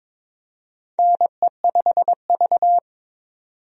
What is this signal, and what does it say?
Morse code, heard as a keyed tone, about 21 wpm, 700 Hz. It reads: NE5V
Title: USA Call Signs - 4 Characters - 22wpm